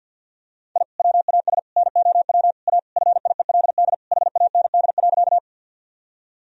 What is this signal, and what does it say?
Morse code, heard as a keyed tone, about 51 wpm, 700 Hz. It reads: I WAS NOW A FIELD HAND.
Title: Sentences from Top 1000 Words - Review Part 2 - 50wpm